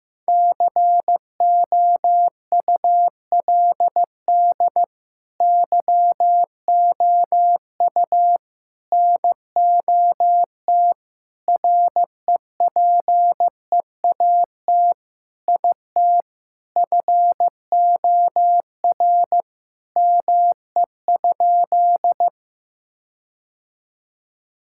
Morse code, heard as a keyed tone, about 16 wpm, 700 Hz. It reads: COULD YOU NOT REPEAT IT FOR ME?
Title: Sentences from Top 1600 Words 15wpm